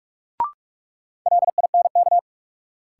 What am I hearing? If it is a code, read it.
Morse code, heard as a keyed tone, about 45 wpm, 700 Hz. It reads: LINK
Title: Sentences from Top 1000 Words - Review Part 7 - New Word Drill - 45wpm